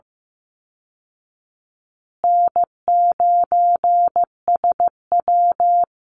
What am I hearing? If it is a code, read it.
Morse code, heard as a keyed tone, about 15 wpm, 700 Hz. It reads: N9SW
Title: Easy Call Signs Repeated 3x - 15wpm